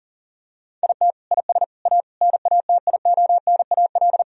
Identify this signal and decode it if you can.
Morse code, heard as a keyed tone, about 37 wpm, 700 Hz. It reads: IT IS A NATIONAL
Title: Sentences from Top 1000 Words - Review Part 4 - 40wpm